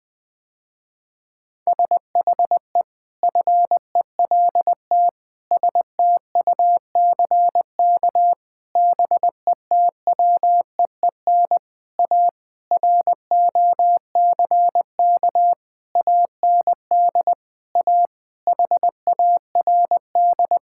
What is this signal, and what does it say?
Morse code, heard as a keyed tone, about 20 wpm, 700 Hz. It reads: SHE FELT STUCK BETWEEN A ROCK AND A HARD
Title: Sentences with 8-Word Phrases - Spoken Beginning - 20wpm